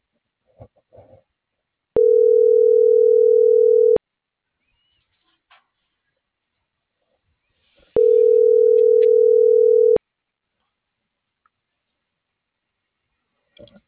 Ringback tone (North America), 440 + 480 Hz.